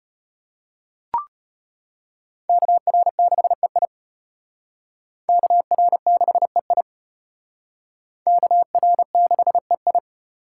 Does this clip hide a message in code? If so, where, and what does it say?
Yes: Morse code from 2.5 to 10.0 s, heard as a keyed tone, about 32 wpm, 700 Hz. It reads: KR6EI KR6EI KR6EI